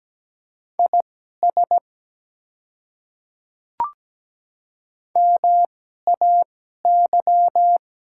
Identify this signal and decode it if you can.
Morse code, heard as a keyed tone, about 17 wpm, spaced as for 12 wpm, 700 Hz. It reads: IS MAY
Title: 2x Character Spacing - Top 100 Words - 17wpm